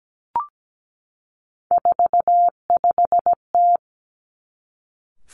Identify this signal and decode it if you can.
Morse code, heard as a keyed tone, about 17 wpm, 700 Hz. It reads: 45T
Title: Three Challenging Letter-Number Combinations 17wpm